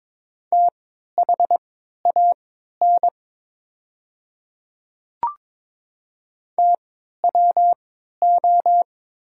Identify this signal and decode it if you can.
Morse code, heard as a keyed tone, about 22 wpm, spaced as for 12 wpm, 700 Hz. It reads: THAN TWO
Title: 3x Character Spacing - Top 500 Words - 22wpm